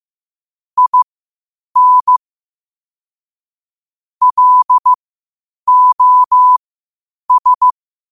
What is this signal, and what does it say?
Morse code, heard as a keyed tone, about 15 wpm, spaced as for 8 wpm, 1000 Hz. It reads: IN LOS
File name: qotdfiles / 8wpm